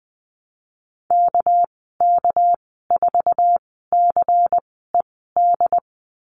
Morse code, heard as a keyed tone, about 20 wpm, spaced as for 15 wpm, 700 Hz. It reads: KK4CED
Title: 219 Lesson 35g - Number 0 - Call Signs 20wpm